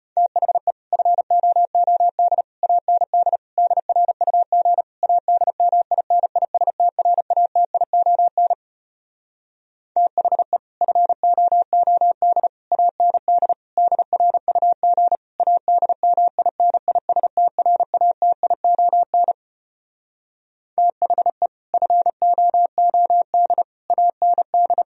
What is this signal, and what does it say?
Morse code, heard as a keyed tone, about 33 wpm, 700 Hz. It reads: THE FOOD AND DRUG ADMINISTRATION THE FOOD AND DRUG ADMINISTRATION THE FOOD AND